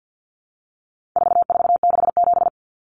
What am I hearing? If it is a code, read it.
Morse code, heard as a keyed tone, about 50 wpm, 700 Hz. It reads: 4467
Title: Numbers - 4-Digit 50wpm